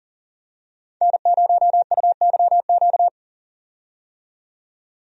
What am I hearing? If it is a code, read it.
Morse code, heard as a keyed tone, about 40 wpm, 700 Hz. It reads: N0UYQ
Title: USA Call Signs - 5 Characters - 40wpm